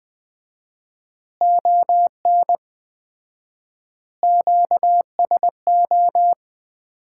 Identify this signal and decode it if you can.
Morse code, heard as a keyed tone, about 20 wpm, 700 Hz. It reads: ON QSO
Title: Common QSO Phrases - 4x Word-Spacing - 20wpm